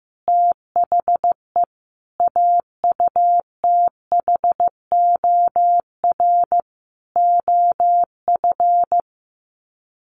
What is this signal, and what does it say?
Morse code, heard as a keyed tone, about 15 wpm, 700 Hz. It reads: THE AUTHOR OF
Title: Sets of 5 Words Encore2 15wpm